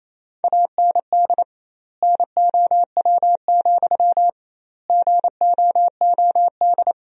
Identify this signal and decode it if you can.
Morse code, heard as a keyed tone, about 27 wpm, 700 Hz. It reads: AND NOW, GOOD
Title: 2x Word-Spacing - Sentences from Top 300 Words - 28wpm